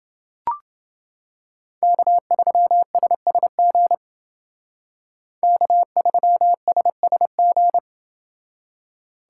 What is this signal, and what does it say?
Morse code, heard as a keyed tone, about 28 wpm, 700 Hz. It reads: K3SSG K3SSG